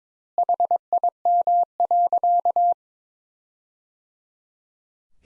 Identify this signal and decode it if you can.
Morse code, heard as a keyed tone, about 22 wpm, 700 Hz. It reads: HIM.
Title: Sentences Top 200 Words - 22wpm